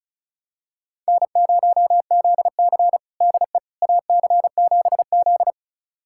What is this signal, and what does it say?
Morse code, heard as a keyed tone, about 34 wpm, 700 Hz. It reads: N0ZC DE AC7Z
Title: USA Callsign DE Callsign - 4 Characters - 35wpm